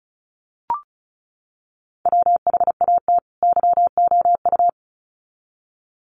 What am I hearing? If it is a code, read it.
Morse code, heard as a keyed tone, about 34 wpm, 700 Hz. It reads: WHAT YOU
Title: Sets of 2 Words 35wpm